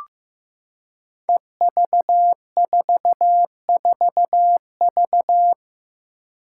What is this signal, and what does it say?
Morse code, heard as a keyed tone, about 15 wpm, 700 Hz. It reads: EV44V
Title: Five Challenging Letter-Number Combinations 15wpm